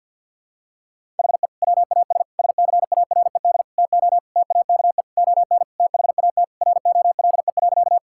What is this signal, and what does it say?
Morse code, heard as a keyed tone, about 52 wpm, 700 Hz. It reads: HE WAS SCARED TO TAKE ON THAT ROLE.
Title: Sentences from Top 1000 Words - Review Part 6 - 50wpm